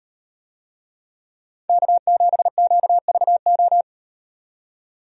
Morse code, heard as a keyed tone, about 38 wpm, 700 Hz. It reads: K7QVO